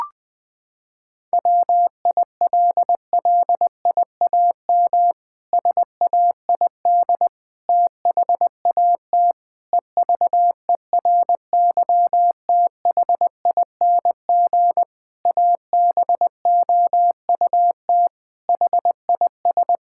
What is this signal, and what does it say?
Morse code, heard as a keyed tone, about 20 wpm, 700 Hz. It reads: WILLIAM SAID THAT EVERYTHING ABOUT HIS
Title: Pangram Challenge v2 - 20wpm